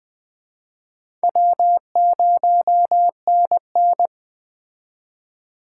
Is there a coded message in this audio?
Yes: Morse code from 1.2 to 4.1 s, heard as a keyed tone, about 22 wpm, 700 Hz. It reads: W0NN